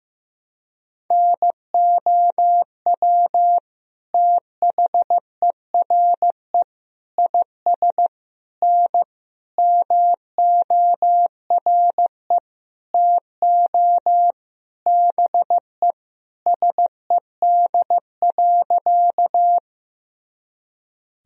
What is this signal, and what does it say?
Morse code, heard as a keyed tone, about 15 wpm, 700 Hz. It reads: NOW THERE IS N MORE TO BE SED.
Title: Sentences from Top 100 Words and Abbreviations 15wpm